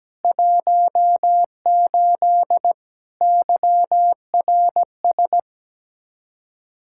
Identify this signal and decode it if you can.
Morse code, heard as a keyed tone, about 17 wpm, 700 Hz. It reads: 18 YRS
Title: Practice QSO - Exchange 3 - More Info - 17wpm